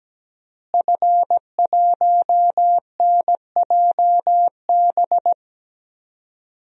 Morse code, heard as a keyed tone, about 17 wpm, 700 Hz. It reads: F1NJB
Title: DXCC-100 Call Signs - 5 Characters - Repeated 2x - 17wpm